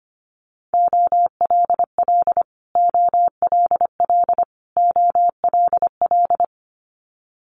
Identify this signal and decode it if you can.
Morse code, heard as a keyed tone, about 25 wpm, 700 Hz. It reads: OLL OLL OLL
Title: Multi-letter Phonemes Repeated 3x 25wpm